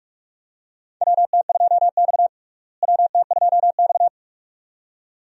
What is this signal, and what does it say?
Morse code, heard as a keyed tone, about 46 wpm, 700 Hz. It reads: WT2X WT2X
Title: USA Call Signs - 4 Characters - Repeated 2x - 45wpm